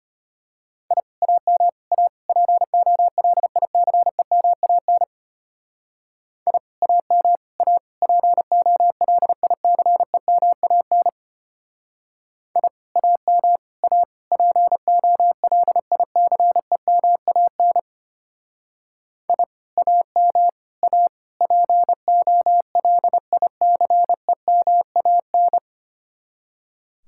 Morse code, heard as a keyed tone, about 33 wpm, 700 Hz. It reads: I AM A POLICEMAN I AM A POLICEMAN I AM A POLICEMAN I AM A POLICEMAN